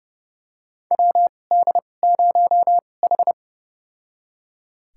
Morse code, heard as a keyed tone, about 31 wpm, spaced as for 22 wpm, 700 Hz. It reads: WD0H
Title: USA Call Signs - 4 Characters - 2x Character Spacing - 30wpm